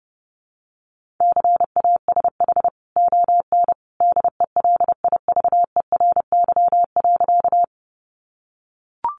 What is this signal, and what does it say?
Morse code, heard as a keyed tone, about 30 wpm, 700 Hz. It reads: CASH ON DELIVERY.
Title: Sentences from Top 1500 Words 30wpm